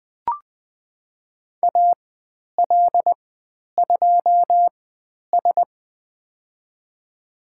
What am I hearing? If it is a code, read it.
Morse code, heard as a keyed tone, about 20 wpm, spaced as for 10 wpm, 700 Hz. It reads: AL2S
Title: 102 Lesson 17g - Letter u - Call Signs 20wpm